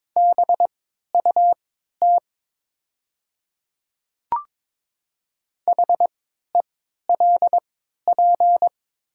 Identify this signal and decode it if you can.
Morse code, heard as a keyed tone, about 22 wpm, spaced as for 12 wpm, 700 Hz. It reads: BUT HELP
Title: 3x Character Spacing - Top 200 Words - 22wpm